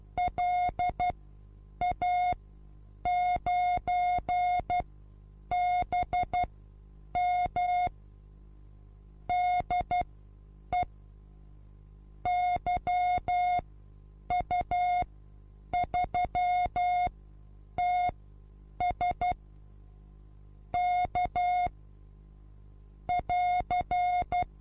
Morse code, heard as a keyed tone, about 12 wpm, spaced as for 8 wpm, 720 Hz. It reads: LA9BM DE YU3TS K <AR>
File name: Onlinelehrgang / morsen / DJ4UF Morsen23